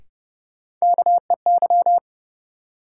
Morse code, heard as a keyed tone, about 30 wpm, 700 Hz. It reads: KEY
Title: Sentences from Top 600 Words - New Word Drill - 30wpm